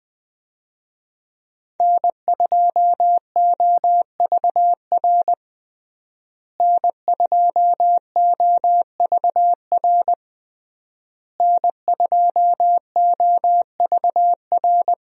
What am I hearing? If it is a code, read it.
Morse code, heard as a keyed tone, about 20 wpm, 700 Hz. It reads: N2OVR N2OVR N2OVR
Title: USA Call Signs - 5 Characters - Repeated 3x - 20wpm